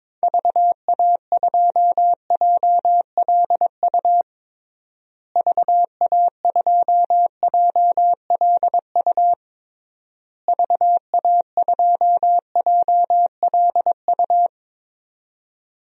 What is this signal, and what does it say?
Morse code, heard as a keyed tone, about 22 wpm, 700 Hz. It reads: VA2JLU VA2JLU VA2JLU
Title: DXCC-100 Call Signs - 6 Characters - Repeated 3x - 22wpm